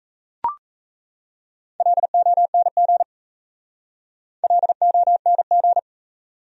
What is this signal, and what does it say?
Morse code, heard as a keyed tone, about 37 wpm, 700 Hz. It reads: LONG LONG